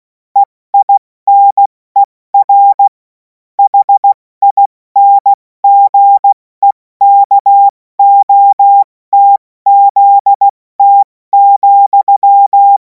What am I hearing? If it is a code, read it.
Morse code, heard as a keyed tone, about 16 wpm, 800 Hz. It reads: EINER HINGEKOTZT,
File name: Morsekurs / Witze / Witz 17. März 2026 70 F80